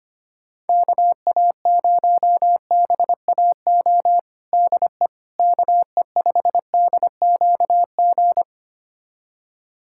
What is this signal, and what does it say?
Morse code, heard as a keyed tone, about 25 wpm, 700 Hz. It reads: KA0BAO DE KE5DQG
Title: USA Callsign DE Callsign 25wpm